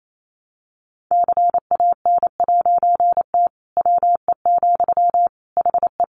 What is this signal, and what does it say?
Morse code, heard as a keyed tone, about 28 wpm, 700 Hz. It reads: CAN'T WE, HE
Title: A Sound Of Thunder 28wpm